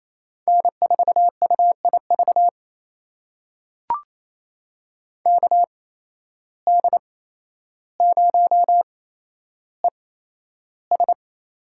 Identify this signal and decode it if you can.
Morse code, heard as a keyed tone, about 28 wpm, spaced as for 12 wpm, 700 Hz. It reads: N4UIV KD0ES
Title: USA Call Signs - 5 Characters - 8x Character Spacing - 28wpm